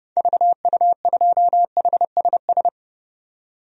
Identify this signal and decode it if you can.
Morse code, heard as a keyed tone, about 29 wpm, 700 Hz. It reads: VU2HSS
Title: DXCC-100 Call Signs - 6 Characters - 30wpm